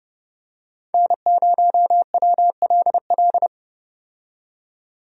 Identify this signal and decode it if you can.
Morse code, heard as a keyed tone, about 32 wpm, 700 Hz. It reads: N0WLL
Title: USA Call Signs - 5 Characters - 30wpm